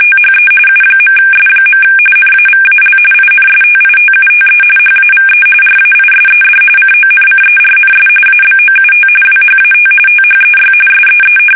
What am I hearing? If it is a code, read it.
RTTY, about 45.45 baud. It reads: THE U.S. NAVAL INSTITUTE HAVE RAISED CONCERNS ABOUT WHAT THEY SEE
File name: fsk850 / usnavy1